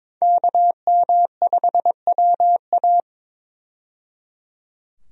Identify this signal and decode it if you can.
Morse code, heard as a keyed tone, about 22 wpm, 700 Hz. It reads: KM5WA
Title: USA Call Signs - 5 Characters - 22wpm